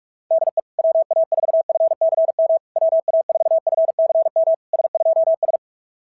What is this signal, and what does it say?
Morse code, heard as a keyed tone, about 46 wpm, 650 Hz. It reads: DE WA4FCM WA4FCM S2S
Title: SOTA QSO Practice 45wpm